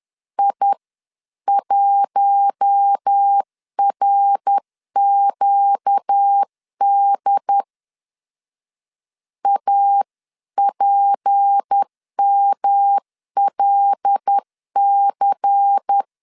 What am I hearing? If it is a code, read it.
Morse code, heard as a keyed tone, about 10 wpm, 800 Hz. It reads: I1RQD APMLC
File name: CW / DET / 1 6 1500C S C 10wpm 4252